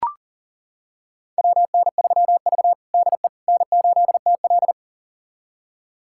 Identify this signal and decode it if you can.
Morse code, heard as a keyed tone, about 37 wpm, 700 Hz. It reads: WN3V DE N8TL
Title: USA Callsign DE Callsign - 4 Characters - 40wpm